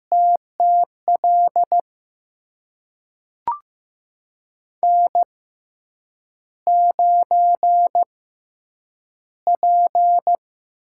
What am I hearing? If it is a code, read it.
Morse code, heard as a keyed tone, about 15 wpm, spaced as for 8 wpm, 700 Hz. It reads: TTL N9P
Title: USA Call Signs - 5 Characters - 6x Character Spacing - 15wpm